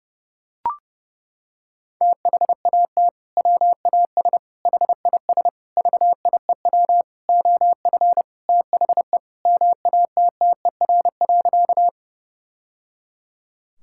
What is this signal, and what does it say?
Morse code, heard as a keyed tone, about 30 wpm, 700 Hz. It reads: THAT WAS HIS VIEW OF THE MATTER.
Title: Sentences from Top 1000 Words - Review Part 6 - 30wpm